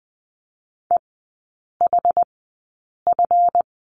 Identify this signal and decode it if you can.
Morse code, heard as a keyed tone, about 20 wpm, spaced as for 8 wpm, 700 Hz. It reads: EHF
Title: 205 Lesson 33e - Letter j - Three Characters 20wpm